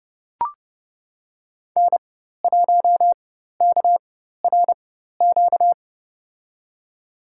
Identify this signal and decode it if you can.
Morse code, heard as a keyed tone, about 30 wpm, spaced as for 14 wpm, 700 Hz. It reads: N1KRQ
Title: USA Call Signs - 5 Characters - 4x Character Spacing - 30wpm